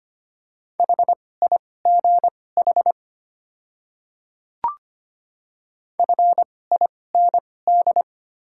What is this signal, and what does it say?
Morse code, heard as a keyed tone, about 25 wpm, spaced as for 18 wpm, 700 Hz. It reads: HIGH FIND
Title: 2x Character Spacing - Top 200 Words - 25wpm